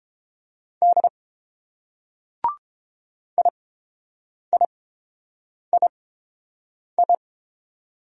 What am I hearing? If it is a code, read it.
Morse code, heard as a keyed tone, about 30 wpm, spaced as for 7 wpm, 700 Hz. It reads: D IIII